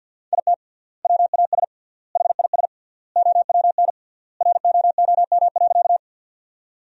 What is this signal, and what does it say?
Morse code, heard as a keyed tone, about 53 wpm, 700 Hz. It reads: IT WAS HIS OWN ROOM.
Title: 3x Word-Spacing - Sentences from Top 300 Words - 50wpm